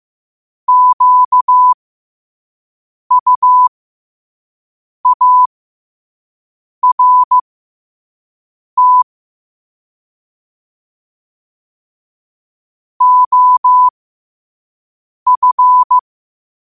Morse code, heard as a keyed tone, about 15 wpm, spaced as for 5 wpm, 1000 Hz. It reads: QUART OF
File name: qotdfiles / 5wpm